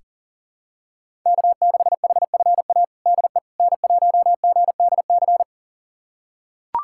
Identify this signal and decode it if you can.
Morse code, heard as a keyed tone, about 40 wpm, 700 Hz. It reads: K6HFA DE N1GDC
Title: USA Callsign DE Callsign - 5 Characters - 40wpm